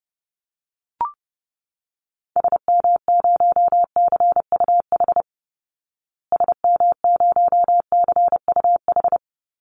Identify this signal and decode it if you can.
Morse code, heard as a keyed tone, about 30 wpm, 700 Hz. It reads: SM0CUH SM0CUH
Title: Call Signs Repeated 3x - 30wpm